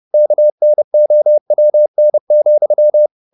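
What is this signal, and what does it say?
Morse code, heard as a keyed tone, about 30 wpm, 600 Hz. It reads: KNOWN,
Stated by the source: Morse code at 30 wpm, the novel Never by Ken Follett in English, from CW med Gnister.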